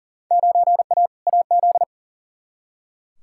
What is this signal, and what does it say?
Morse code, heard as a keyed tone, about 40 wpm, 700 Hz. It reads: 9A AZ
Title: Field Day Exchanges - 40wpm